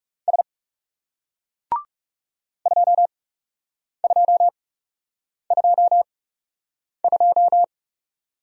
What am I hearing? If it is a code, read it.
Morse code, heard as a keyed tone, about 40 wpm, spaced as for 8 wpm, 700 Hz. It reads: S 2222